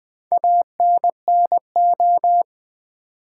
Morse code, heard as a keyed tone, about 20 wpm, 700 Hz. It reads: ANNO
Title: Build A Word - 4 Letters At A Time - 8-Letter Words - 20wpm